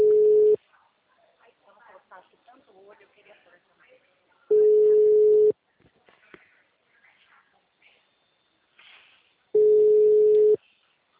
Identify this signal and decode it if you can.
Ringback tone (Europe, 425 Hz), with speech.